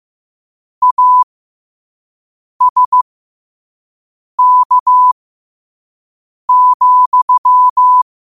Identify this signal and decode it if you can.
Morse code, heard as a keyed tone, about 15 wpm, spaced as for 5 wpm, 1000 Hz. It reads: ASK,
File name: qotdfiles / 5wpm